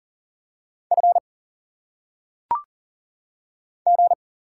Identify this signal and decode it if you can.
Morse code, heard as a keyed tone, about 40 wpm, 700 Hz. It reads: F G
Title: Single Letters - Phonetic Pronunciation 40wpm